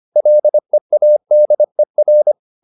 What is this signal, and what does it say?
Morse code, heard as a keyed tone, about 25 wpm, 600 Hz. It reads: LEADER